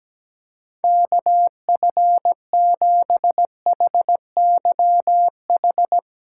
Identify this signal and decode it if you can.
Morse code, heard as a keyed tone, about 17 wpm, 700 Hz. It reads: KF7HYH
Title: USA Callsign DE Callsign - 6 Characters - 17wpm